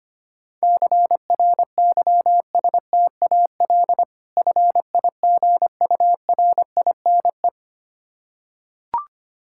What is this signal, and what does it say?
Morse code, heard as a keyed tone, about 25 wpm, 700 Hz. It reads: CRYSTAL FIGURINE
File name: practice / 25wpm / Difficult Words with Hints - 25wpm